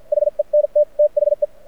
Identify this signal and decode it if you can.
Morse code, heard as a keyed tone, about 52 wpm, spaced as for 44 wpm, 590 Hz. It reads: HE NT THE
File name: archived / CRSnet